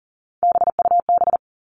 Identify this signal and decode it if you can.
Morse code, heard as a keyed tone, about 40 wpm, 700 Hz. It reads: BUB
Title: All Cut Numbers - 3-Digit 40wpm